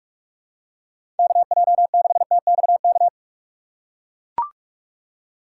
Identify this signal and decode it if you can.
Morse code, heard as a keyed tone, about 43 wpm, 690 Hz. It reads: KJ6TXK
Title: USA Call Signs - 6 Characters - 2x Character Spacing - 45wpm